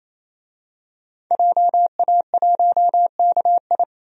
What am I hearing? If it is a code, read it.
Morse code, heard as a keyed tone, about 28 wpm, 700 Hz. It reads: JA1KI
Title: DXCC-100 Call Signs - 5 Characters - 28wpm